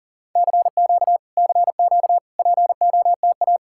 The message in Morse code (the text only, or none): CQ CQ POTA